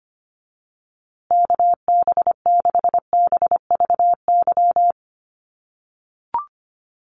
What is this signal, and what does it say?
Morse code, heard as a keyed tone, about 25 wpm, 700 Hz. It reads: KB6BVY
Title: USA Call Signs Repeated 2x 25wpm